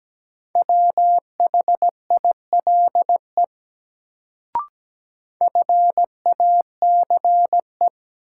Morse code, heard as a keyed tone, about 17 wpm, 700 Hz. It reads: WHILE FACE
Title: Sentences from Top 300 Words - New Word Drill - 17wpm